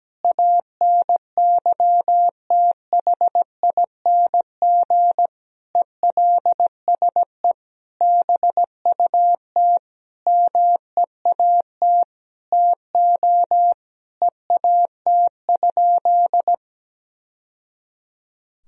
Morse code, heard as a keyed tone, about 17 wpm, 700 Hz. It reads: ANYTHING ELSE BUT MEAT TO EAT?
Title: Sentences from Top 1000 Words - Review Part 3 - 17wpm